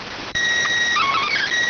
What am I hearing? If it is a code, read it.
SSTV (slow-scan television), PD 290.